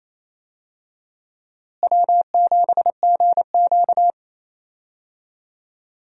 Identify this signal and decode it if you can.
Morse code, heard as a keyed tone, about 28 wpm, 700 Hz. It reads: W7GQ
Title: USA Call Signs - 4 Characters - Repeated 3x - 28wpm